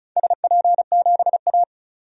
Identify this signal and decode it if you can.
Morse code, heard as a keyed tone, about 34 wpm, 700 Hz. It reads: SP7A
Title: DXCC-100 Call Signs - 4 Characters - 35wpm